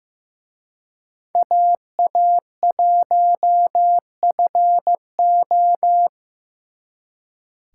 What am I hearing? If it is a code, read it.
Morse code, heard as a keyed tone, about 18 wpm, 700 Hz. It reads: AA1FO
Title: Speed Racing - USA 2x2 Call Signs 15wpm